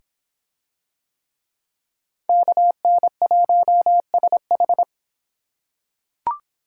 Morse code, heard as a keyed tone, about 25 wpm, 700 Hz. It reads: KN1SH